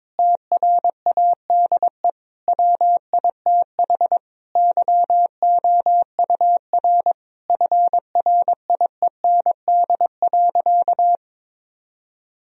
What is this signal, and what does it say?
Morse code, heard as a keyed tone, about 22 wpm, 700 Hz. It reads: TRADE WITH YOUR FRIEND.
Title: Sentences from Top 1000 Words - Review Part 4 - 22wpm